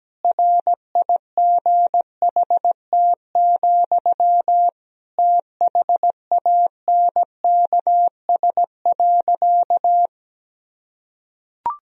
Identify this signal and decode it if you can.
Morse code, heard as a keyed tone, about 17 wpm, 700 Hz. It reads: RIGHT, THANKS.
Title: Sentences from Top 1000 Words - Review Part 3 - 17wpm